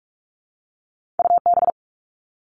Morse code, heard as a keyed tone, about 46 wpm, 700 Hz. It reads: UB
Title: All Cut Numbers - 2-Digit 45wpm